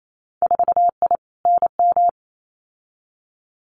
Morse code, heard as a keyed tone, about 28 wpm, 700 Hz. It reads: 4I NM
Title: Winter Field Day Exchanges Encore 28wpm